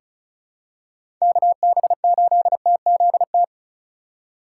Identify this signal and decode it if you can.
Morse code, heard as a keyed tone, about 36 wpm, 700 Hz. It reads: KB8TZT
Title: USA Call Signs - 3x Character Spacing - 35wpm